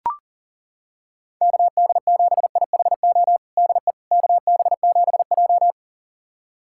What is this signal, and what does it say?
Morse code, heard as a keyed tone, about 40 wpm, 700 Hz. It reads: KD7IHO DE KB7J
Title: USA Callsign DE Callsign 40wpm